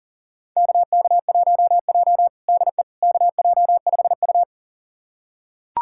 Morse code, heard as a keyed tone, about 40 wpm, 700 Hz. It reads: KK1J DE KJ5U
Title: USA Callsign DE Callsign - 4 Characters - 40wpm